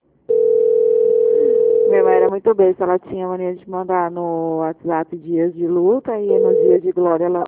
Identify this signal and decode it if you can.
Ringback tone (North America), 440 + 480 Hz, with speech.
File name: out- -2006- - - .10